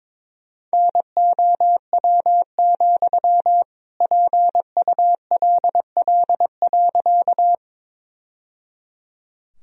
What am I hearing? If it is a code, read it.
Morse code, heard as a keyed tone, about 22 wpm, 700 Hz. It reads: NOW, PULL.
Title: Sentences from Top 1000 Words - Review Part 2 - 22wpm